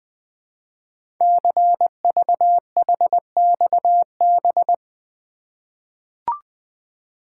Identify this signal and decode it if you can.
Morse code, heard as a keyed tone, about 20 wpm, 700 Hz. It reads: CVHXB
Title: Code Groups - 5 Letters - 20wpm